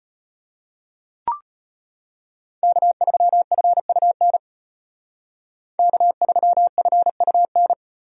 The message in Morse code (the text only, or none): K3FUN K3FUN